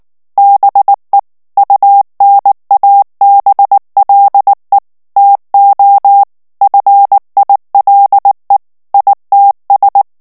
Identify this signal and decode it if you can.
Morse code, heard as a keyed tone, about 19 wpm, 800 Hz. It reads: BE UNABLE TO FILE ITS
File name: ANGLAIS texte clair 3 vitesse 18 mots minute